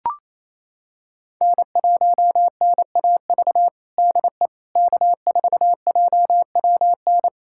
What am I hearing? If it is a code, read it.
Morse code, heard as a keyed tone, about 28 wpm, 700 Hz. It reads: N1NAV DE K4JWN
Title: USA Callsign DE Callsign - 5 Characters - 28wpm